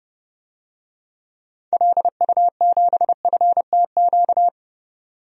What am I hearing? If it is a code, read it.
Morse code, heard as a keyed tone, about 30 wpm, 700 Hz. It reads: LU7FTQ
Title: DXCC-100 Call Signs - 6 Characters - 30wpm